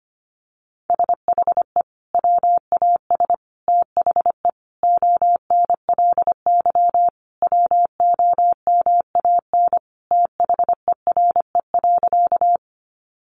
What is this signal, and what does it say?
Morse code, heard as a keyed tone, about 25 wpm, 700 Hz. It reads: SHE WAS THE ONLY WOMAN THERE.
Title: Sentences from Top 1100 Words 25wpm